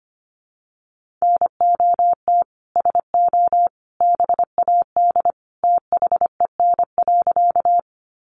Morse code, heard as a keyed tone, about 25 wpm, 700 Hz. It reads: NOT SO BAD THEN.
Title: Sentences from Top 700 Words 25wpm